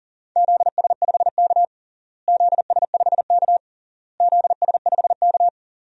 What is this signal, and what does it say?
Morse code, heard as a keyed tone, about 40 wpm, 700 Hz. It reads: ZS5K ZS5K ZS5K
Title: DXCC-100 Call Signs - 4 Characters - Repeated 3x - 40wpm